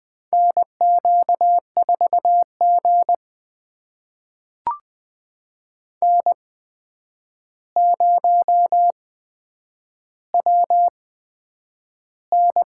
Morse code, heard as a keyed tone, about 20 wpm, spaced as for 9 wpm, 700 Hz. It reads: NQ4G N0WN
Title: USA Call Signs - 4 Characters - 8x Character Spacing - 20wpm